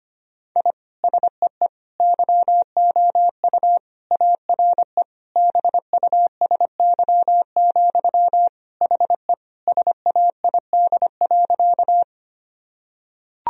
Morse code, heard as a keyed tone, about 25 wpm, 700 Hz. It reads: I SEE YOU ARE BUSY, HE SAID.
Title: Sentences from Top 1000 Words - Review Part 3 - 25wpm